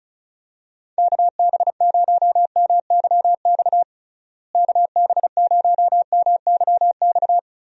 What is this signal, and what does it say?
Morse code, heard as a keyed tone, about 34 wpm, 700 Hz. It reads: KB0MYX KB0MYX
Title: USA Call Signs - 6 Characters - Repeated 3x - 35wpm